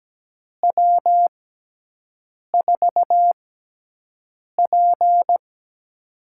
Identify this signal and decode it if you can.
Morse code, heard as a keyed tone, about 17 wpm, spaced as for 6 wpm, 700 Hz. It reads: W4P
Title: USA Call Signs - 4 Characters - 6x Character Spacing - 17wpm